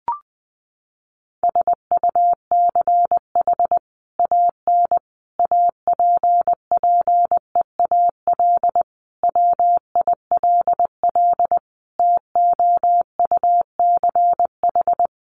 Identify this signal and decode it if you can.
Morse code, heard as a keyed tone, about 20 wpm, 700 Hz. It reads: SUCH AN APPEAL WILL TOUCH
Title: Sentences from Top 1300 Words 20wpm